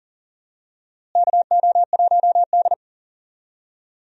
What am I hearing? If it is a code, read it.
Morse code, heard as a keyed tone, about 40 wpm, 700 Hz. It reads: KO1D
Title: USA Call Signs - 4 Characters - Repeated 3x - 40wpm